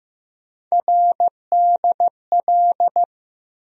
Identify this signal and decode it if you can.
Morse code, heard as a keyed tone, about 15 wpm, 700 Hz. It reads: RDL
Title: USA Call Sign Postfixes - 3 Letters 15wpm